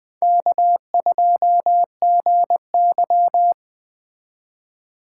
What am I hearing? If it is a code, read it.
Morse code, heard as a keyed tone, about 20 wpm, 700 Hz. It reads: K2GY
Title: Speed Racing - USA 1x2 Call Signs 17wpm